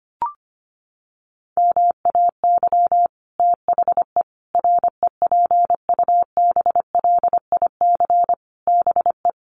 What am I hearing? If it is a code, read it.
Morse code, heard as a keyed tone, about 25 wpm, 700 Hz. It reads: MAY THE REPUBLIC BE
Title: Sentences from Top 1500 Words 25wpm